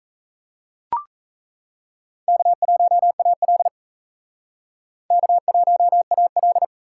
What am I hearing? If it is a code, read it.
Morse code, heard as a keyed tone, about 37 wpm, 700 Hz. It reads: K1AL K1AL